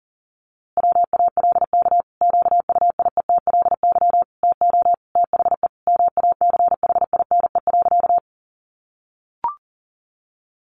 Morse code, heard as a keyed tone, about 40 wpm, 700 Hz. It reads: WALK QUIETLY TO THE MACHINE.
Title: A Sound Of Thunder 40wpm